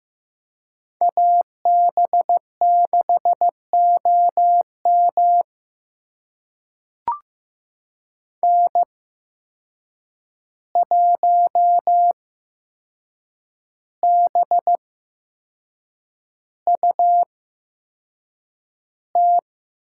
Morse code, heard as a keyed tone, about 15 wpm, spaced as for 6 wpm, 700 Hz. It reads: AB6OM N1BUT
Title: USA Call Signs - 5 Characters - 8x Character Spacing - 15wpm